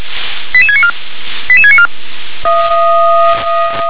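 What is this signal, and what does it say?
Five-tone selective call (ZVEI), heard as quick runs of five tones, tones of about 70 ms. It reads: 88694 88694
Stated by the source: ZVEI five-tone selective call 88711, received via SDR.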